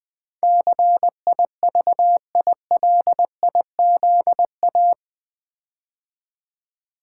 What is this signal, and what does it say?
Morse code, heard as a keyed tone, about 20 wpm, 700 Hz. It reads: CIVILIZA
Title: Build A Word - 4 Letters At A Time - 12-Letter Words - 20wpm